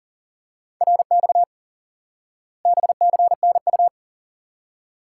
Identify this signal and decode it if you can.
Morse code, heard as a keyed tone, about 40 wpm, 700 Hz. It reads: RX BCNU
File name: IQE Course 3 - Lesson 89 - YAGI - Two Elements - 40wpm